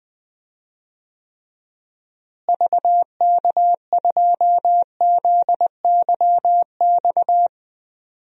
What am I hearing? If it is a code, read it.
Morse code, heard as a keyed tone, about 20 wpm, 700 Hz. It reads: VK2ZYX
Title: DXCC-100 Call Signs - 6 Characters - Repeated 3x - 20wpm